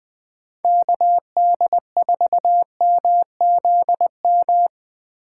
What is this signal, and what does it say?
Morse code, heard as a keyed tone, about 20 wpm, 700 Hz. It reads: KD4MZM
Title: 2023 POTA Activator Call Signs - 20wpm